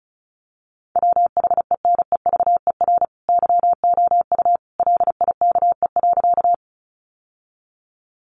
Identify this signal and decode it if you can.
Morse code, heard as a keyed tone, about 34 wpm, 700 Hz. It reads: WHENEVER YOU LIKE.
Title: Sentences from Top 1200 Words 35wpm